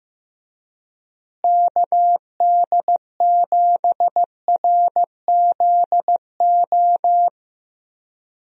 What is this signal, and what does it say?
Morse code, heard as a keyed tone, about 15 wpm, 700 Hz. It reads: KD7RZO
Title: USA Call Signs - 6 Characters - Repeated 3x - 15wpm